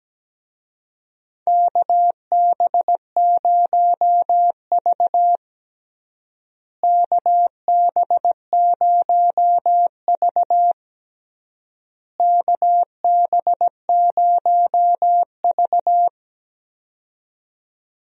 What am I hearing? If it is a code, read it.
Morse code, heard as a keyed tone, about 17 wpm, 700 Hz. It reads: KB0V KB0V KB0V
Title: USA Call Signs - 4 Characters - Repeated 3x - 17wpm